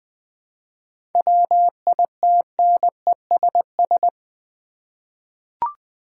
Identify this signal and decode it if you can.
Morse code, heard as a keyed tone, about 20 wpm, 700 Hz. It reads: WITNESS
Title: Seven Letter Words - 20wpm